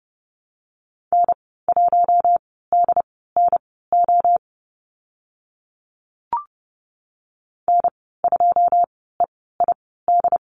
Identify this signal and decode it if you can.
Morse code, heard as a keyed tone, about 30 wpm, spaced as for 17 wpm, 700 Hz. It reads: N1DNO N2EID
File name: Lesson 15g - Number 5 - Call Signs 30wpm